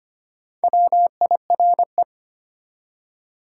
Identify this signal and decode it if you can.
Morse code, heard as a keyed tone, about 34 wpm, spaced as for 30 wpm, 700 Hz. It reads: WIRE
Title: Speed-Racing Top 180 QSO Elements 25wpm